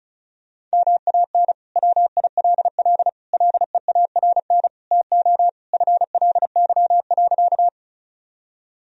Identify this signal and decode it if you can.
Morse code, heard as a keyed tone, about 34 wpm, 700 Hz. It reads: MAN WILL LEARN TO FLY.
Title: Sentences from Top 1000 Words - Review Part 2 - 35wpm